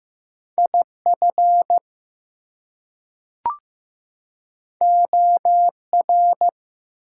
Morse code, heard as a keyed tone, about 15 wpm, 700 Hz. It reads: IF OR
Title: Two Letter Words - 15wpm